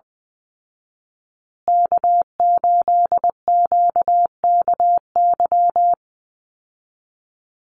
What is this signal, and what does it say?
Morse code, heard as a keyed tone, about 20 wpm, 700 Hz. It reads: K8QKY
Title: Call Signs Repeated 2x - 20wpm